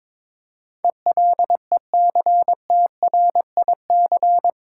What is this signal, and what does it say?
Morse code, heard as a keyed tone, about 22 wpm, 700 Hz. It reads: ELECTRIC
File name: Difficult Words with Hints - 22wpm